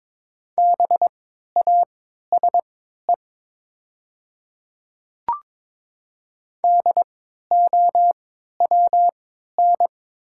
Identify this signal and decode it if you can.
Morse code, heard as a keyed tone, about 22 wpm, spaced as for 12 wpm, 700 Hz. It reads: BASE DOWN
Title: 3x Character Spacing - Top 300 Words - 22wpm